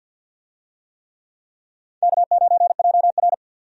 Morse code, heard as a keyed tone, about 47 wpm, 700 Hz. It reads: K9JR
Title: USA Call Signs - 4 Characters - 50wpm